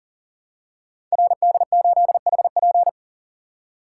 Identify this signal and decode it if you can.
Morse code, heard as a keyed tone, about 38 wpm, 700 Hz. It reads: RD8HP
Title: Code Groups - 5 Letter:Numbers - 40wpm